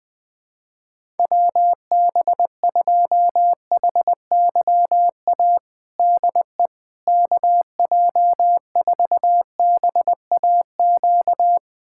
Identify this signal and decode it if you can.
Morse code, heard as a keyed tone, about 21 wpm, 700 Hz. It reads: WB2HYA DE KJ4BAQ
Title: USA Callsign DE Callsign - 6 Characters - 20wpm